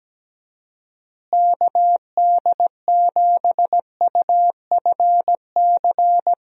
Morse code, heard as a keyed tone, about 17 wpm, 700 Hz. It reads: KD7UFC
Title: USA Call Signs - 6 Characters - Repeated 3x - 17wpm